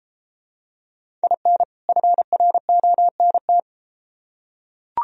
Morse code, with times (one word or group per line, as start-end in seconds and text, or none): 1.2-1.6 IN
1.9-3.6 FRONT